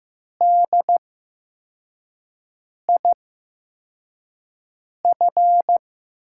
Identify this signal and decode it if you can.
Morse code, heard as a keyed tone, about 15 wpm, spaced as for 4 wpm, 700 Hz. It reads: DIF
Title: USA Call Signs - 5 Characters - 8x Character Spacing - 15wpm